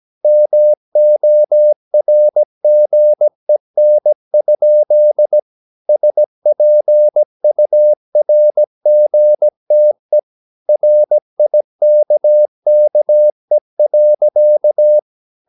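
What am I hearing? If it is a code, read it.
Morse code, heard as a keyed tone, about 17 wpm, 600 Hz. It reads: MORGEN? SPURGTE RIKKE.
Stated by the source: Slow (QRS) Morse practice at 17 wpm, from CW med Gnister.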